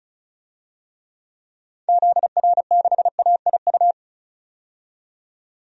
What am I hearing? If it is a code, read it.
Morse code, heard as a keyed tone, about 36 wpm, 700 Hz. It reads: ZR6AIU
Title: DXCC-100 Call Signs - 6 Characters - 35wpm